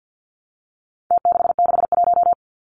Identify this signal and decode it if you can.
Morse code, heard as a keyed tone, about 50 wpm, 700 Hz. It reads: T661
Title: Numbers - Contest Style - 4-Digit 50wpm